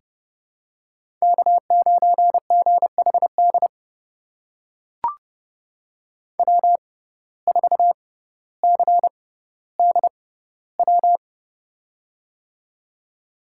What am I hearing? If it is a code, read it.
Morse code, heard as a keyed tone, about 32 wpm, spaced as for 16 wpm, 700 Hz. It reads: K9GHD W4CDW
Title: USA Call Signs - 5 Characters - 6x Character Spacing - 30wpm